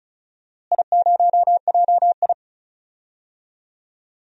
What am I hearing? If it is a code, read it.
Morse code, heard as a keyed tone, about 36 wpm, 690 Hz. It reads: I0JI
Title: DXCC-100 Call Signs - 4 Characters - Repeated 3x - 35wpm